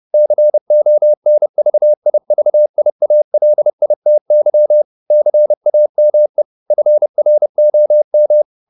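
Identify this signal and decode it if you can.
Morse code, heard as a keyed tone, about 30 wpm, 600 Hz. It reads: CONVIVIALITY CAME FROM
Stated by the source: Morse code at 30 wpm, the novel Never by Ken Follett in English, from CW med Gnister.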